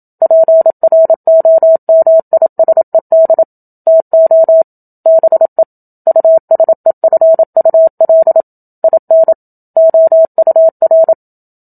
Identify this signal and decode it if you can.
Morse code, heard as a keyed tone, about 27 wpm, 650 Hz. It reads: PROMISED TO BE USEFUL IN OUR
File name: Book Two / 01-Chapter 1 - 27 WPM